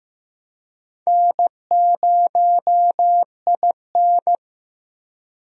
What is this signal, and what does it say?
Morse code, heard as a keyed tone, about 15 wpm, 700 Hz. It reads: N0IN
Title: USA Call Signs - 4 Characters - Repeated 2x - 15wpm